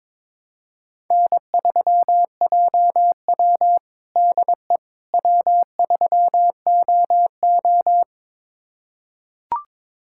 Morse code, heard as a keyed tone, about 21 wpm, 700 Hz. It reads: N3JW DE W3OO
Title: USA Callsign DE Callsign - 4 Characters - 22wpm